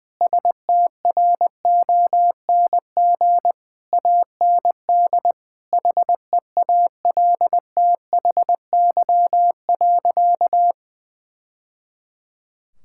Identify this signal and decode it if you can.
Morse code, heard as a keyed tone, about 21 wpm, 700 Hz. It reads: STRONG AND HEALTHY.
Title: Sentences from Top 1000 Words - Review Part 5 - 20wpm